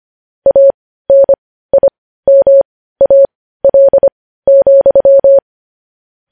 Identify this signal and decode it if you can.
Morse code, heard as a keyed tone, about 25 wpm, spaced as for 15 wpm, 550 Hz. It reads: ANIMAL,
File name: podcast / News headlines-1x-25x15